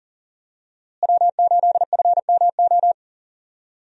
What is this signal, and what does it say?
Morse code, heard as a keyed tone, about 40 wpm, 700 Hz. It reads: W8FMO
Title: USA Call Signs - 5 Characters - Repeated 2x - 40wpm